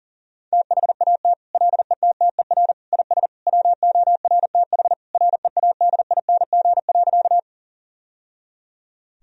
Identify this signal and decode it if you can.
Morse code, heard as a keyed tone, about 40 wpm, 700 Hz. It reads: THAT LETTER IS WORTH READING.
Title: Sentences from Top 1000 Words - Review Part 3 - 40wpm